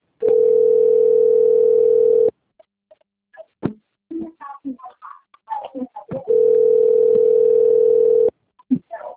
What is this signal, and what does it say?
Ringback tone (North America), 440 + 480 Hz.